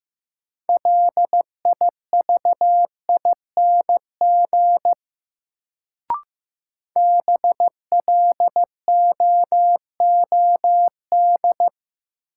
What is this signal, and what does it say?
Morse code, heard as a keyed tone, about 15 wpm, 700 Hz. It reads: LIVING BLOOD
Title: Sentences from Top 1100 Words - New Word Drill Alt - 15wpm